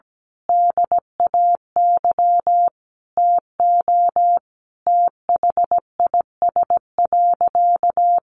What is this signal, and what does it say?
Morse code, heard as a keyed tone, about 17 wpm, 700 Hz. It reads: DAY TO THIS.
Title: Sentences from Top 1400 Words 17wpm